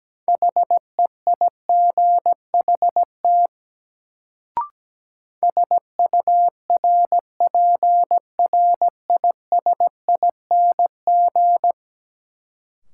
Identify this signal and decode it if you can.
Morse code, heard as a keyed tone, about 17 wpm, 700 Hz. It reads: HEIGHT SURPRISING
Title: Sets of 5 Words - New Word Drill - 17wpm